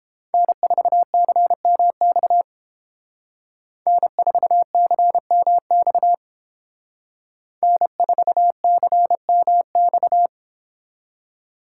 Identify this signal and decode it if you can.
Morse code, heard as a keyed tone, about 30 wpm, 700 Hz. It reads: N4CMX N4CMX N4CMX